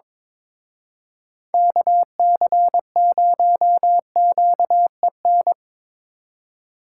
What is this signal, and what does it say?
Morse code, heard as a keyed tone, about 25 wpm, 700 Hz. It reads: KC0QEN
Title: Speed Racing - USA Call Signs 22wpm